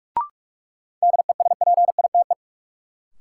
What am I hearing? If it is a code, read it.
Morse code, heard as a keyed tone, about 41 wpm, 700 Hz. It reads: DESPITE
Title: Sets of 4 Words - New Word Drill - 45wpm